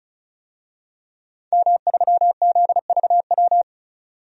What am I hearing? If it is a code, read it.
Morse code, heard as a keyed tone, about 36 wpm, 700 Hz. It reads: M3ZVW
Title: DXCC-100 Call Signs - 5 Characters - 35wpm